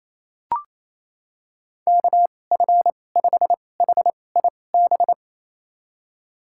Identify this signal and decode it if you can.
Morse code, heard as a keyed tone, about 28 wpm, spaced as for 20 wpm, 700 Hz. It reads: KF5HIB
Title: USA Call Signs - 2x Character Spacing - 28wpm